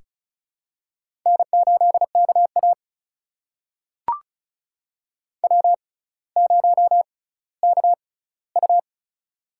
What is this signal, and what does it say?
Morse code, heard as a keyed tone, about 36 wpm, spaced as for 18 wpm, 700 Hz. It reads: N8KA W0KU
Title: USA Call Signs - 4 Characters - 6x Character Spacing - 35wpm